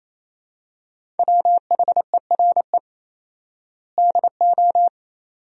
Morse code, heard as a keyed tone, about 28 wpm, 700 Hz. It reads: WHERE DO
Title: 4x Word-Spacing - Sentences from Top 200 Words - 28wpm